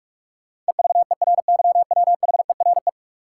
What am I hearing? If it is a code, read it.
Morse code, heard as a keyed tone, about 42 wpm, 700 Hz. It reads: EVERYWHERE
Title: Sentences from Top 1600 Words - New Word Drill Alt - 45wpm